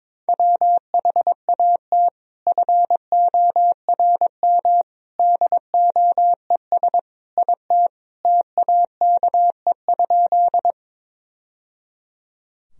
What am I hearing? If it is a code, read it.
Morse code, heard as a keyed tone, about 22 wpm, 700 Hz. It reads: WHAT FORM DOES IT TAKE?
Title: Sentences from Top 1000 Words - Review Part 1 - 22wpm